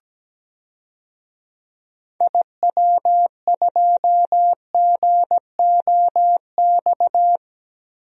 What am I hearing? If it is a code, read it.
Morse code, heard as a keyed tone, about 17 wpm, 700 Hz. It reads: IW2GOX
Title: DXCC-100 Call Signs - 6 Characters - Repeated 2x - 17wpm